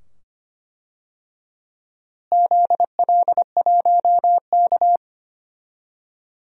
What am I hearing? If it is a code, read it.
Morse code, heard as a keyed tone, about 25 wpm, 700 Hz. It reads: ZL1K
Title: DXCC-100 Call Signs - 4 Characters - Repeated 2x - 25wpm